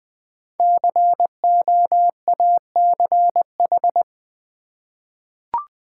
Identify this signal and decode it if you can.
Morse code, heard as a keyed tone, about 20 wpm, 700 Hz. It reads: COACH
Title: Five Letter Words - 20wpm